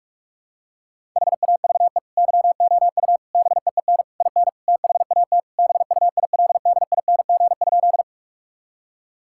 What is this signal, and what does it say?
Morse code, heard as a keyed tone, about 46 wpm, 700 Hz. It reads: HAVE YOU BEEN IN THAT BUILDING?
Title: Sentences from Top 1000 Words - Review Part 5 - 45wpm